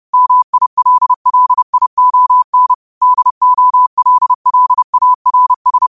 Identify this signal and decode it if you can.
Morse code, heard as a keyed tone, about 30 wpm, 1000 Hz. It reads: MILLION DOLLARS
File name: qotdfiles / 30wpm